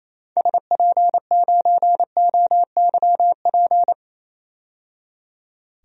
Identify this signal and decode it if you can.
Morse code, heard as a keyed tone, about 28 wpm, 700 Hz. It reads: SP9OYP
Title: DXCC-100 Call Signs - 6 Characters - 28wpm